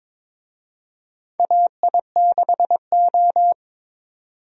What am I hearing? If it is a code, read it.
Morse code, heard as a keyed tone, about 22 wpm, 700 Hz. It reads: AI6O
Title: USA Call Signs - 4 Characters - 22wpm